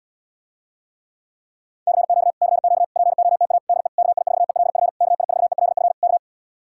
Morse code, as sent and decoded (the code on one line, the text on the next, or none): -.-. --.- / -.-. --.- / .--. --- - .- / -.. / -.- .. ....- . .-.. ..-. / -.- .. ....- . .-.. ..-. / -.-
CQ CQ POTA DE KI4ELF KI4ELF K